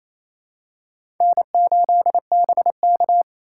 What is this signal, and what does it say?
Morse code, heard as a keyed tone, about 28 wpm, 700 Hz. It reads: N8BK
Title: USA Call Signs - 4 Characters - 8x Character Spacing - 28wpm